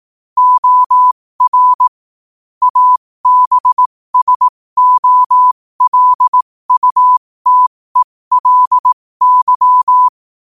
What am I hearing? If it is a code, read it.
Morse code, heard as a keyed tone, about 18 wpm, spaced as for 15 wpm, 1000 Hz. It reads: OR ABSOLUTELY
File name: 15wpm